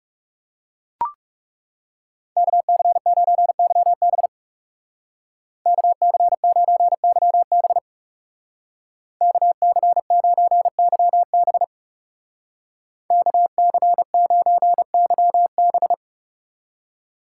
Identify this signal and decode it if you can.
Morse code, heard as a keyed tone, about 38 wpm, 700 Hz. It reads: KC9YB KC9YB KC9YB KC9YB